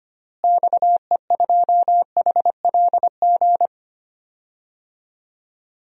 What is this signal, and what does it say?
Morse code, heard as a keyed tone, about 25 wpm, 700 Hz. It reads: XE2HLG
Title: DXCC-100 Call Signs - 6 Characters - Repeated 2x - 25wpm